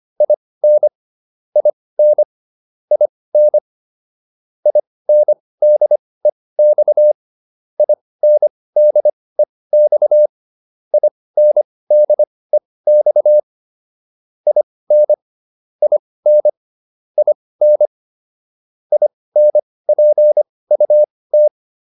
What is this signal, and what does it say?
Morse code, heard as a keyed tone, about 25 wpm, spaced as for 18 wpm, 600 Hz. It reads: IN IN IN INDEX INDEX INDEX IN IN IN INPUT